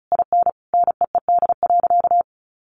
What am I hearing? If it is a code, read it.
Morse code, heard as a keyed tone, about 34 wpm, 700 Hz. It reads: IN NEED.
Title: Sentences from Top 1200 Words 35wpm